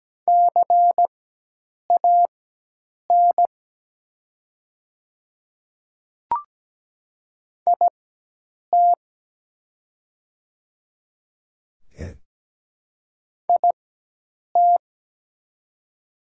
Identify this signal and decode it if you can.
Morse code, heard as a keyed tone, about 17 wpm, spaced as for 8 wpm, 700 Hz. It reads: CAN IT IT
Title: 4x Character Spacing - Top 100 Words - 17wpm